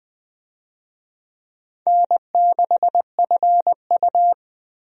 Morse code, heard as a keyed tone, about 20 wpm, 700 Hz. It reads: N6FU
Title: USA Call Signs - 4 Characters - Repeated 3x - 20wpm